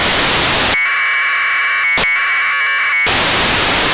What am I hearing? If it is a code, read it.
Packet radio (AX.25, as in APRS).